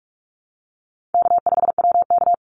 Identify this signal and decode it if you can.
Morse code, heard as a keyed tone, about 44 wpm, 700 Hz. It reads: K5WK
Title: Easy Call Signs - 45wpm